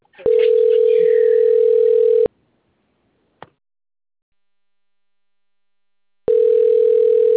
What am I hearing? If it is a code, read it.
Ringback tone (North America), 440 + 480 Hz.
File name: out- - - - - .75604